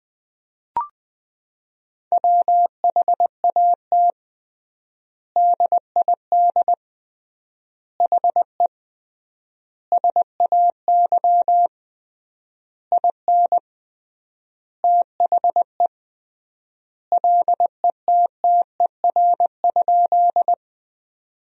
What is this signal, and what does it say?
Morse code, heard as a keyed tone, about 20 wpm, 700 Hz. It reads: WHAT DID HE SAY IN THE LETTER?
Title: 3x Word-Spacing - Sentences from Top 300 Words - 20wpm